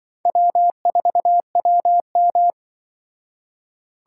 Morse code, heard as a keyed tone, about 24 wpm, 700 Hz. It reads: W4WM
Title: Speed Racing - USA 1x2 Call Signs 20wpm